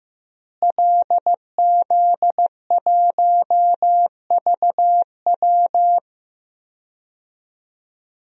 Morse code, heard as a keyed tone, about 15 wpm, 700 Hz. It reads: LZ1VW
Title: DXCC-100 Call Signs - 5 Characters - Repeated 3x - 15wpm